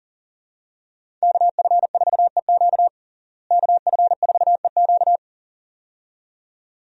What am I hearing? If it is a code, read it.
Morse code, heard as a keyed tone, about 37 wpm, 700 Hz. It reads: KF4EQ KF4EQ
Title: USA Call Signs - 5 Characters - Repeated 2x - 40wpm